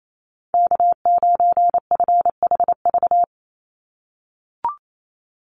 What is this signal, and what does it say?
Morse code, heard as a keyed tone, about 28 wpm, 700 Hz. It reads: K9FHV
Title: USA Call Signs 28wpm